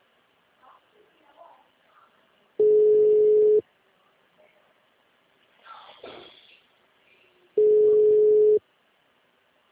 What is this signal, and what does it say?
Ringback tone (Europe, 425 Hz), with speech.